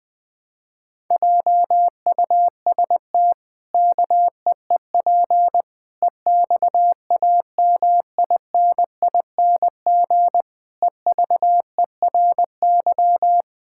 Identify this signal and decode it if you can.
Morse code, heard as a keyed tone, about 21 wpm, 700 Hz. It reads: JUST KEEP EXAMINING EVERY
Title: Pangram Challenge v2 - 20wpm